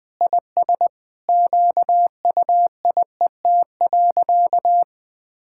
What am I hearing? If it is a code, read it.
Morse code, heard as a keyed tone, about 20 wpm, 700 Hz. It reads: IS QUIET.
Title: Sentences with 3-Word Phrases - 20wpm